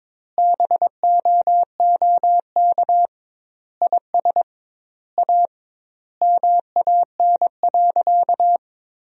Morse code, heard as a keyed tone, about 22 wpm, 700 Hz. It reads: BOOK IS A MAN.
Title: 2x Word-Spacing - Sentences from Top 500 Words - 22wpm